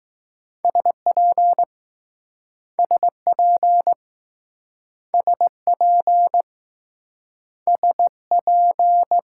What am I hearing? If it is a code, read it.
Morse code, heard as a keyed tone, about 19 wpm, 700 Hz. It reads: SP SP SP S P